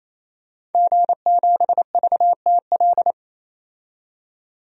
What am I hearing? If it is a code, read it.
Morse code, heard as a keyed tone, about 28 wpm, 700 Hz. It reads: G7VTL
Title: DXCC-100 Call Signs - 5 Characters - 28wpm